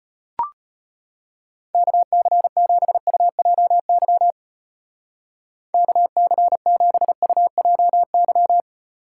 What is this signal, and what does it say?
Morse code, heard as a keyed tone, about 32 wpm, 700 Hz. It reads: KC7UJY KC7UJY